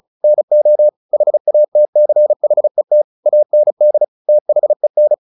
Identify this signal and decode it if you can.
Morse code, heard as a keyed tone, about 34 wpm, 600 Hz. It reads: NO HATCHET AND THEN
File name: the old man and the sea 2 35wpm